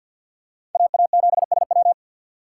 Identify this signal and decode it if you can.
Morse code, heard as a keyed tone, about 46 wpm, 700 Hz. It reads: AA7SW
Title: USA Call Signs - 5 Characters - 2x Character Spacing - 50wpm